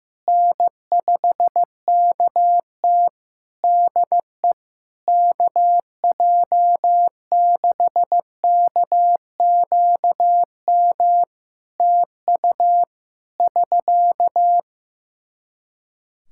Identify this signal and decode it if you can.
Morse code, heard as a keyed tone, about 15 wpm, 700 Hz. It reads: N5KT DE KJ6KQM TU <SK>
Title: Practice QSO - Exchange 4 - Ending - 15wpm